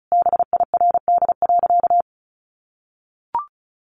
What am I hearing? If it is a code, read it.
Morse code, heard as a keyed tone, about 34 wpm, 700 Hz. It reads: BIRD.
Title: Sentences from Top 1500 Words 35wpm